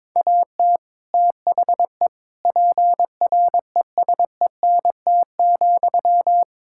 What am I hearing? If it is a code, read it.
Morse code, heard as a keyed tone, about 22 wpm, 700 Hz. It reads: AT THE PRESENT,
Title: Sentences from Top 1000 Words - Review Part 5 - 22wpm